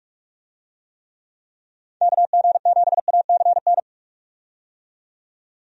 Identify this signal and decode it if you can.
Morse code, heard as a keyed tone, about 46 wpm, 700 Hz. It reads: KG7ACN
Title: USA Call Signs - 6 Characters - 45wpm